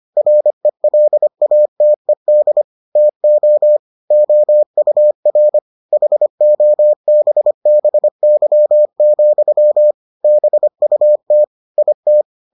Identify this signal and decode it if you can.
Morse code, heard as a keyed tone, about 25 wpm, 600 Hz. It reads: RELATED TO OUR HOBBY, BUT IT